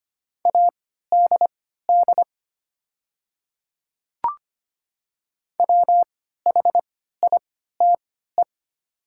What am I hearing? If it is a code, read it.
Morse code, heard as a keyed tone, about 25 wpm, spaced as for 14 wpm, 700 Hz. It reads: ADD WHITE
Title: 3x Character Spacing - Top 300 Words - 25wpm